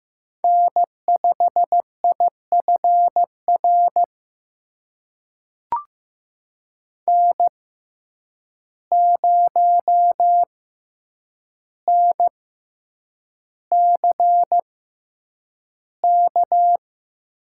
Morse code, heard as a keyed tone, about 15 wpm, spaced as for 8 wpm, 700 Hz. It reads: N5IFR N0NCK
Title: USA Call Signs - 6x Character Spacing - 15wpm